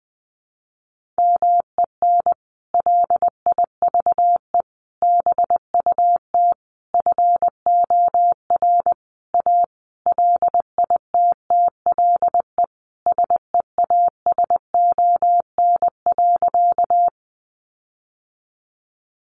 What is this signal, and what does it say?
Morse code, heard as a keyed tone, about 20 wpm, 700 Hz. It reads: MEN LIVE BUT FOR A LITTLE SEASON.
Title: Sentences from Top 1400 Words 20wpm